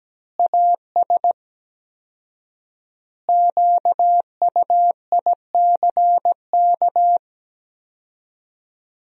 Morse code, heard as a keyed tone, about 17 wpm, 700 Hz. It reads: AS QUICK
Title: 4x Word-Spacing - Sentences from Top 500 Words - 17wpm